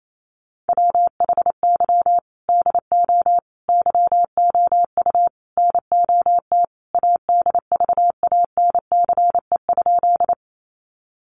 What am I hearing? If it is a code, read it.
Morse code, heard as a keyed tone, about 28 wpm, 700 Hz. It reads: WHY DO YOU NOT ADVANCE?
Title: Sentences from Top 1400 Words 28wpm